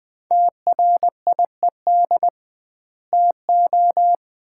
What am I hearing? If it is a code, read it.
Morse code, heard as a keyed tone, about 20 wpm, 700 Hz. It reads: TRIED TO
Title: 2x Word-Spacing - Sets of 2 Words - 20wpm